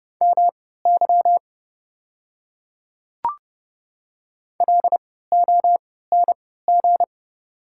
Morse code, heard as a keyed tone, about 30 wpm, spaced as for 17 wpm, 700 Hz. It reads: MY LONG
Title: 3x Character Spacing - Top 200 Words - 30wpm